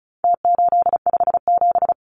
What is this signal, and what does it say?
Morse code, heard as a keyed tone, about 34 wpm, 700 Hz. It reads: T857
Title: Numbers - Contest Style - 4-Digit 35wpm